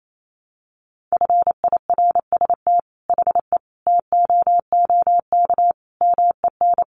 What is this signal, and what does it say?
Morse code, heard as a keyed tone, about 28 wpm, 700 Hz. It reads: FIRST HE TOOK MEN
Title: Sentences from Top 1500 Words 28wpm